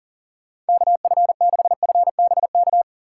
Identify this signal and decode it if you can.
Morse code, heard as a keyed tone, about 50 wpm, 700 Hz. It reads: KF6FBK
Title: Speed Racing - USA 2x3 Call Signs 40wpm